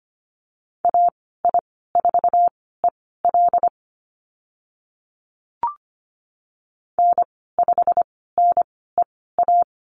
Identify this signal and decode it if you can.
Morse code, heard as a keyed tone, about 25 wpm, spaced as for 16 wpm, 700 Hz. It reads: AI4EL N5NEA
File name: Lesson 15g - Number 5 - Call Signs 25wpm